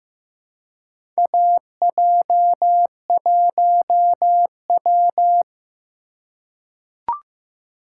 Morse code, heard as a keyed tone, about 16 wpm, 700 Hz. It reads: AJ1W
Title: USA Call Signs - 4 Characters - 4x Character Spacing - 15wpm